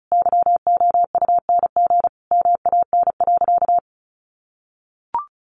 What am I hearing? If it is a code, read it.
Morse code, heard as a keyed tone, about 34 wpm, 700 Hz. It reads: YOUNG MAN.
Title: Sentences from Top 1000 Words 35wpm